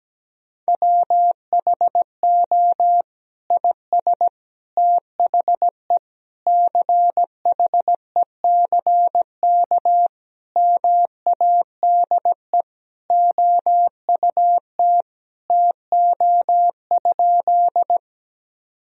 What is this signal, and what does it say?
Morse code, heard as a keyed tone, about 17 wpm, 700 Hz. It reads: WHO IS THE CHECK MADE OUT TO?
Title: Sentences from Top 1000 Words - Review Part 3 - 17wpm